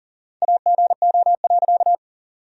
Morse code, heard as a keyed tone, about 40 wpm, 700 Hz. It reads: AGO.
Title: 4x Word-Spacing - Sentences from Top 500 Words - 40wpm